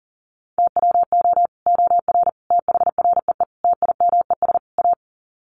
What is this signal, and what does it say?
Morse code, heard as a keyed tone, about 40 wpm, 700 Hz. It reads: TWO OR THREE TIMES A
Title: Sets of 5 Words Encore2 40wpm